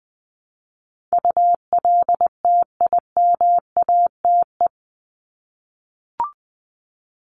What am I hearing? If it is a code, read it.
Morse code, heard as a keyed tone, about 20 wpm, 700 Hz. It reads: ULTIMATE
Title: LY Suffix - Top 100 Words with Base Words 20wpm